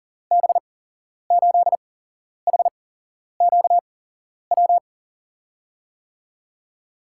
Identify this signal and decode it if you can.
Morse code, heard as a keyed tone, about 40 wpm, spaced as for 11 wpm, 700 Hz. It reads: B8HQW
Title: USA Call Signs - 6 Characters - 8x Character Spacing - 40wpm